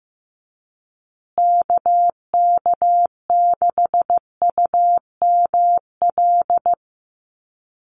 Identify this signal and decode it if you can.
Morse code, heard as a keyed tone, about 15 wpm, 700 Hz. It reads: KK6UML
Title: USA Call Signs Repeated 3x 15wpm